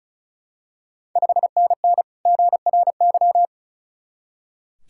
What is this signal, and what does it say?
Morse code, heard as a keyed tone, about 36 wpm, 700 Hz. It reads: 5NN GRY
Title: Ontario QSO Party Exchange 35wpm